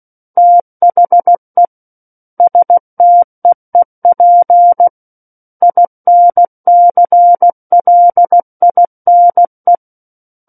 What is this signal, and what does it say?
Morse code, heard as a keyed tone, about 16 wpm, 700 Hz. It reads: THE STEEP INCLINE
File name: Book One / 07-Chapter 7 - 16 WPM